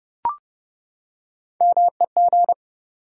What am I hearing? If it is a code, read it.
Morse code, heard as a keyed tone, about 30 wpm, 700 Hz. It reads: MEG
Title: USA Common Names - Max 3 Letters - 30wpm